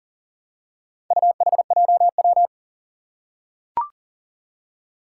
Morse code, heard as a keyed tone, about 38 wpm, 700 Hz. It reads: UHJW
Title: Four Challenging Letter-Number Combinations 40wpm